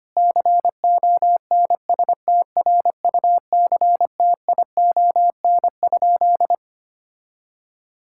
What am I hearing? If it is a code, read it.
Morse code, heard as a keyed tone, about 25 wpm, 700 Hz. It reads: CONSTRUCTION?
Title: Sentences from Top 1600 Words 25wpm